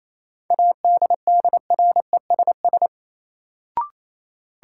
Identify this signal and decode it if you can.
Morse code, heard as a keyed tone, about 28 wpm, 700 Hz. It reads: ADDRESS
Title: Sentences from Top 1000 Words - New Word Drill - 28wpm